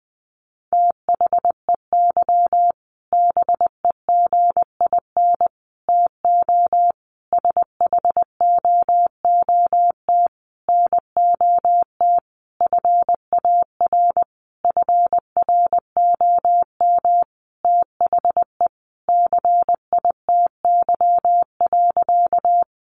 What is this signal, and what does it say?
Morse code, heard as a keyed tone, about 20 wpm, 700 Hz. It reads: THEY BEGIN TO SHOOT NOT FAR FROM THE CITY.
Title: Sentences from Top 1200 Words 20wpm